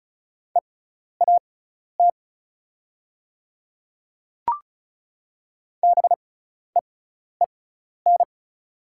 Morse code, heard as a keyed tone, about 36 wpm, spaced as for 12 wpm, 700 Hz. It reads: EAT BEEN
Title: 6x Character Spacing - Top 500 Words - 35wpm